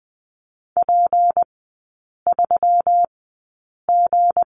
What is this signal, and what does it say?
Morse code, heard as a keyed tone, about 20 wpm, spaced as for 8 wpm, 700 Hz. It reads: P3G
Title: 159 Lesson 26e - Letter g - Three Characters 20wpm